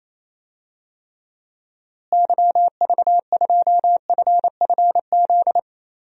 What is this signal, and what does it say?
Morse code, heard as a keyed tone, about 28 wpm, 700 Hz. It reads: YV2FFZ
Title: DXCC-100 Call Signs - 6 Characters - Repeated 3x - 28wpm